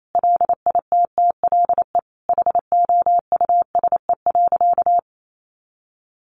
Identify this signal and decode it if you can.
Morse code, heard as a keyed tone, about 28 wpm, 700 Hz. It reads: LITTLE HOUSE.
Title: Sentences from Top 200 Words 28wpm